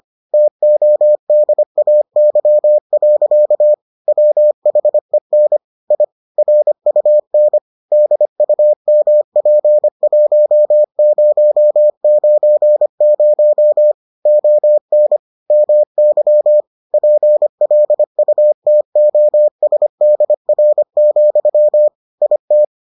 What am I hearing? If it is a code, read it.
Morse code, heard as a keyed tone, about 25 wpm, 600 Hz. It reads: TODAY. WHEN I RUN DUMP1090 ON MY PLUTOSDR, IT